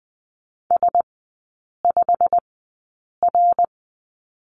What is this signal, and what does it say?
Morse code, heard as a keyed tone, about 20 wpm, spaced as for 8 wpm, 700 Hz. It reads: S5R
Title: 087 Lesson 15e - Number 5 - Three Characters 20wpm